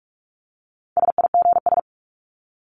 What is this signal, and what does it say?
Morse code, heard as a keyed tone, about 44 wpm, 700 Hz. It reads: SIGS
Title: Abbreviations 45wpm